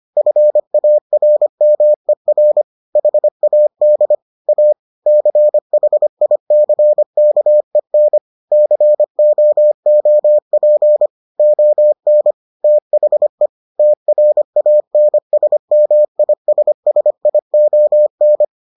Morse code, heard as a keyed tone, about 25 wpm, 600 Hz. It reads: FARMER HAD A CHICKEN COOP ON THE TRANSMISSION